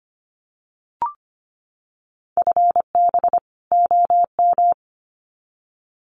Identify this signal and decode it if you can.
Morse code, heard as a keyed tone, about 25 wpm, 700 Hz. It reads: FB OM
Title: Common QSO Phrases 25wpm